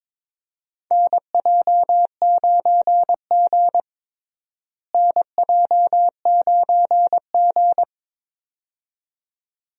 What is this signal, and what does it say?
Morse code, heard as a keyed tone, about 23 wpm, 700 Hz. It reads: NJ9G NJ9G
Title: USA Call Signs - 4 Characters - Repeated 2x - 22wpm